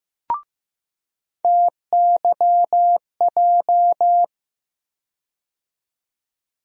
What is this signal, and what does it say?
Morse code, heard as a keyed tone, about 15 wpm, 700 Hz. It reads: TYJ
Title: USA Call Sign Postfixes - 3 Letters 15wpm